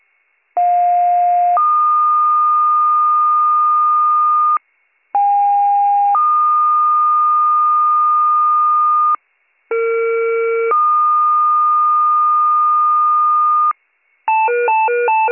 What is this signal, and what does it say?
Two-tone page, heard as a one-second tone, then a three-second tone, about 710 then 1220 Hz, about 810 then 1220 Hz, about 460 then 1220 Hz. It is heard from 0.6 to 13.7 s.